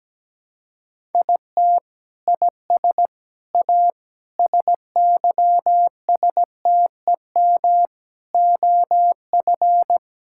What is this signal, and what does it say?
Morse code, heard as a keyed tone, about 17 wpm, 700 Hz. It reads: IT IS A SYSTEM OF
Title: Sentences from Top 1000 Words - Review Part 6 - 17wpm